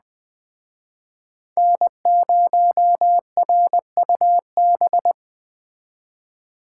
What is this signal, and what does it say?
Morse code, heard as a keyed tone, about 22 wpm, 700 Hz. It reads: N0RUB